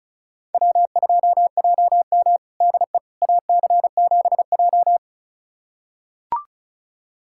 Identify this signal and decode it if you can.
Morse code, heard as a keyed tone, about 36 wpm, 700 Hz. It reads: W2JM DE AC7J
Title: USA Callsign DE Callsign 35wpm